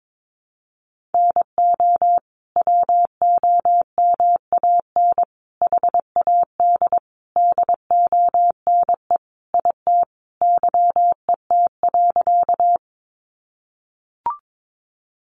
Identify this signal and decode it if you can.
Morse code, heard as a keyed tone, about 22 wpm, 700 Hz. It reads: NO WOMAN HAD DONE IT YET.
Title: Sentences from Top 1100 Words 22wpm